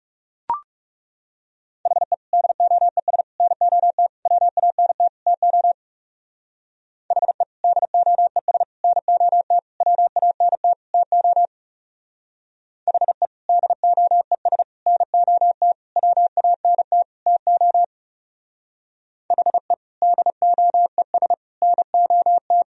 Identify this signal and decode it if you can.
Morse code, heard as a keyed tone, about 34 wpm, 700 Hz. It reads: HE DOES NOT WANT TO HE DOES NOT WANT TO HE DOES NOT WANT TO HE DOES NOT
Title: Speed-Racing - Sets of 5 Words 30wpm